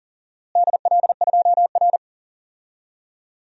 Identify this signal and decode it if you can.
Morse code, heard as a keyed tone, about 40 wpm, 700 Hz. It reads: DL2R
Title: DXCC-100 Call Signs - 4 Characters - Repeated 3x - 40wpm